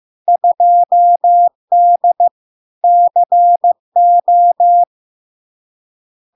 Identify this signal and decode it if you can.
Morse code, heard as a keyed tone, about 15 wpm, 700 Hz. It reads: 2D CO
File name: LARK-FD-2022-Log-in-CW-15wpm-700-Hz